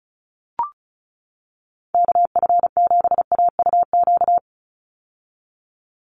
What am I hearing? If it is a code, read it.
Morse code, heard as a keyed tone, about 34 wpm, 700 Hz. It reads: KF7AUQ
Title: USA Call Signs 35wpm